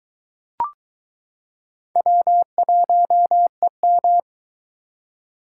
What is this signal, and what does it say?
Morse code, heard as a keyed tone, about 22 wpm, 700 Hz. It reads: W1EM